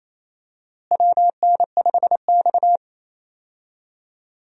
Morse code, heard as a keyed tone, about 28 wpm, 700 Hz. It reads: WN5X
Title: USA Call Signs - 4 Characters - Repeated 3x - 28wpm